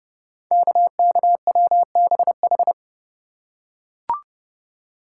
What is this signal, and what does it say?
Morse code, heard as a keyed tone, about 30 wpm, 700 Hz. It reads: KKWBH
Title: Code Groups - 5 Letters - 30wpm